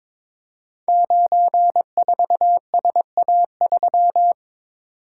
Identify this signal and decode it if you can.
Morse code, heard as a keyed tone, about 22 wpm, 700 Hz. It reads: 94SA3
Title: Code Groups - 5 Letter:Numbers - 22wpm